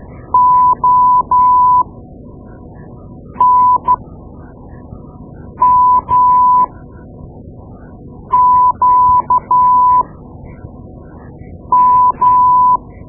Morse code, heard as a keyed tone, about 13 wpm, spaced as for 4 wpm, 1000 Hz. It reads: ONMQM
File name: Morse Lessons / Morse Lessons / Tape-2B